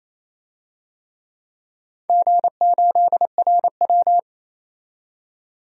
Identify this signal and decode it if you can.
Morse code, heard as a keyed tone, about 28 wpm, 700 Hz. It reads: G8RW
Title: DXCC-100 Call Signs - 4 Characters - 28wpm